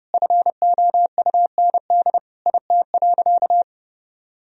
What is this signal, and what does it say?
Morse code, heard as a keyed tone, about 30 wpm, 700 Hz. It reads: FOUND IT.
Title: Sentences from Top 1000 Words - Review Part 4 - 30wpm